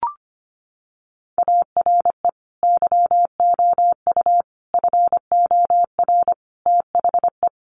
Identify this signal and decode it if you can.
Morse code, heard as a keyed tone, about 25 wpm, 700 Hz. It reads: ARE YOU FOR THE
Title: Sentences from Top 1100 Words 25wpm